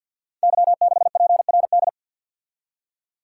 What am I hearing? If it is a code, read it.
Morse code, heard as a keyed tone, about 53 wpm, 700 Hz. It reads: Y6PRD
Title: Code Groups - 5 Letter:Numbers - 50wpm